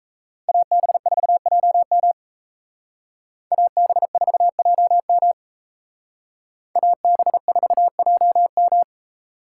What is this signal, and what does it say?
Morse code, heard as a keyed tone, about 36 wpm, 700 Hz. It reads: AB4JM AB4JM AB4JM